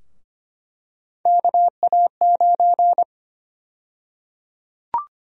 Morse code, heard as a keyed tone, about 25 wpm, 700 Hz. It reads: KA9
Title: USA Call Sign Prefixes - 25wpm